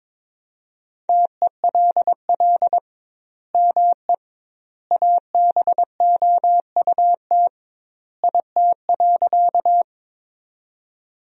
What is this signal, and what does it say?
Morse code, heard as a keyed tone, about 22 wpm, 700 Hz. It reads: TELL ME ABOUT IT.
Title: 2x Word-Spacing - Sentences from Top 200 Words - 22wpm